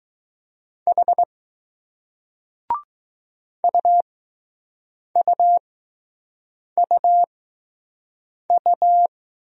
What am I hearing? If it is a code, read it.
Morse code, heard as a keyed tone, about 20 wpm, spaced as for 6 wpm, 700 Hz. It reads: H UUUU